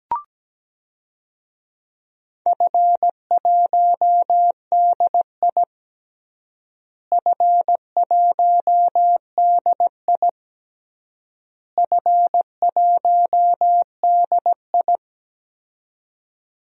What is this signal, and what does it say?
Morse code, heard as a keyed tone, about 17 wpm, 700 Hz. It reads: F1DI F1DI F1DI
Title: DXCC-100 Call Signs - 4 Characters - Repeated 3x - 17wpm